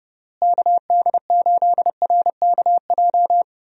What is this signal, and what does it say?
Morse code, heard as a keyed tone, about 30 wpm, 700 Hz. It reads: KD8RKJ
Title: USA Call Signs - 6 Characters - Repeated 3x - 30wpm